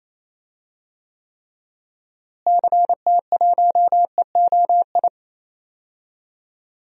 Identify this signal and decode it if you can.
Morse code, heard as a keyed tone, about 28 wpm, 700 Hz. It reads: CT1EOI
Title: DXCC-100 Call Signs - 6 Characters - Repeated 3x - 28wpm